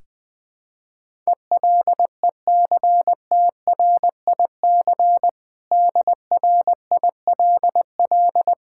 Morse code, heard as a keyed tone, about 20 wpm, 700 Hz. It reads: ELECTRIC DRILL
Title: Difficult Words with Hints - 20wpm